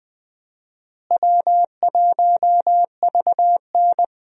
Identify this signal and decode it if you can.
Morse code, heard as a keyed tone, about 20 wpm, 700 Hz. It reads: W1VN
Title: USA Call Signs - 4 Characters - Repeated 2x - 20wpm